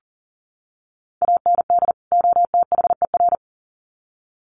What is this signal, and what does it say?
Morse code, heard as a keyed tone, about 40 wpm, 700 Hz. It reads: AND OTHER
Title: Sets of 2 Words 40wpm